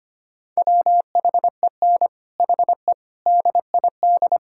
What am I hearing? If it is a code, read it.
Morse code, heard as a keyed tone, about 25 wpm, 700 Hz. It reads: WHEN HE DID
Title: Sentences from Top 1600 Words 25wpm